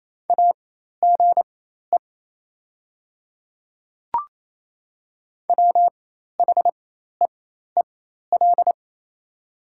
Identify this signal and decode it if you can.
Morse code, heard as a keyed tone, about 27 wpm, spaced as for 13 wpm, 690 Hz. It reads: AGE WHEEL
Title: 4x Character Spacing - Top 500 Words - 28wpm